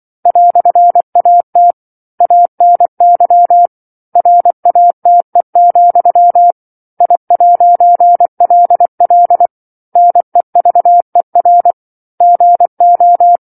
Morse code, heard as a keyed tone, about 24 wpm, 700 Hz. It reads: "AT ANY RATE, I'LL NEVER GO